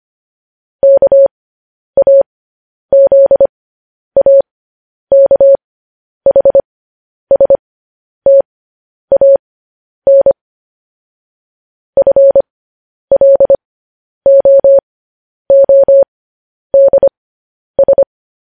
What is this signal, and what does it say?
Morse code, heard as a keyed tone, about 25 wpm, spaced as for 10 wpm, 550 Hz. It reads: KAZAKHSTAN FLOODS